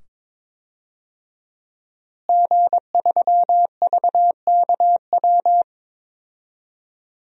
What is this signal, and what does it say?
Morse code, heard as a keyed tone, about 22 wpm, 700 Hz. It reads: G3VKW
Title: DXCC-100 Call Signs - 5 Characters - Repeated 2x - 22wpm